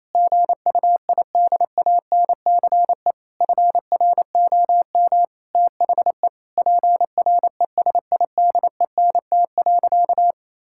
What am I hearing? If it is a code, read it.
Morse code, heard as a keyed tone, about 28 wpm, 700 Hz. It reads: GUIDANCE FROM THE PRESIDENT.
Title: Sentences from Top 1600 Words 28wpm